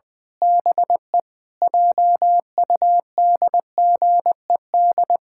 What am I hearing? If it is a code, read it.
Morse code, heard as a keyed tone, about 20 wpm, 700 Hz. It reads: BE JUDGED
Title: Pangram Challenge v2 - 20wpm